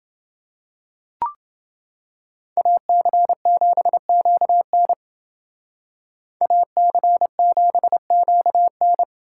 Morse code, heard as a keyed tone, about 28 wpm, 700 Hz. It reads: AC7QN AC7QN